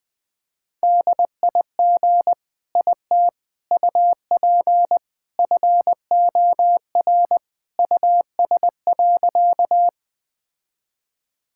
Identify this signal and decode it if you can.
Morse code, heard as a keyed tone, about 21 wpm, 700 Hz. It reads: DIG IT UP FOR US.
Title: Sentences from Top 1600 Words 20wpm